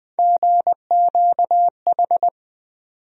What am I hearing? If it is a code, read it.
Morse code, heard as a keyed tone, about 20 wpm, 700 Hz. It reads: GQH
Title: USA Call Sign Postfixes - 3 Letters 20wpm